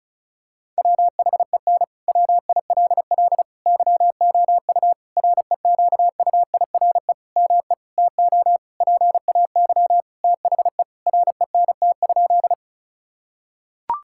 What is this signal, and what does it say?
Morse code, heard as a keyed tone, about 34 wpm, 700 Hz. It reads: WHEN WILL YOU REQUIRE ME TO PAY THE RENT?
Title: Sentences from Top 1000 Words - Review Part 6 - 35wpm